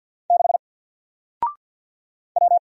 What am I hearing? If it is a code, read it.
Morse code, heard as a keyed tone, about 48 wpm, 700 Hz. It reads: 6 W
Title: Single Letter-Number Alt 50wpm